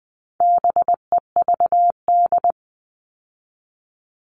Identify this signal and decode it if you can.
Morse code, heard as a keyed tone, about 20 wpm, 700 Hz. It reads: BEVD
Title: All Cut Numbers - 4-Digit 20wpm